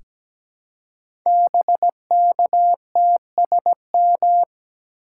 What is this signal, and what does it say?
Morse code, heard as a keyed tone, about 17 wpm, 700 Hz. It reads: BKTSM
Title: Code Groups - 5 Letters - 17wpm